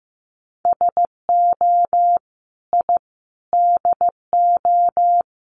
Morse code, heard as a keyed tone, about 15 wpm, 700 Hz. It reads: SO I DO
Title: Sets of 5 Words 15wpm